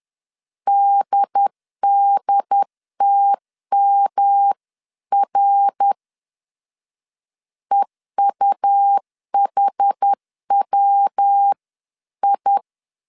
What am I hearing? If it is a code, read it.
Morse code, heard as a keyed tone, about 10 wpm, 800 Hz. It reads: DDTMR EUHWI
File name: 10wpm